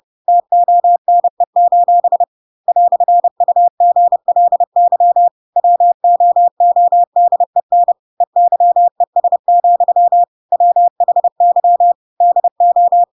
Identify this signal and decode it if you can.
Morse code, heard as a keyed tone, about 30 wpm, 700 Hz. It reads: TONE: "UGLY WOODEN EYES, WHY DO
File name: Pinocchio / Pinocchio ch3 30 30 ik1hge